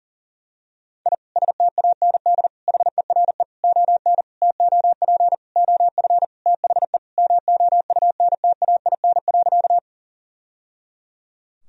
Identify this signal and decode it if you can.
Morse code, heard as a keyed tone, about 40 wpm, 700 Hz. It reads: I STAND HERE ON TOP OF THE MOUNTAIN.
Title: Sentences from Top 1000 Words - Review Part 1 - 40wpm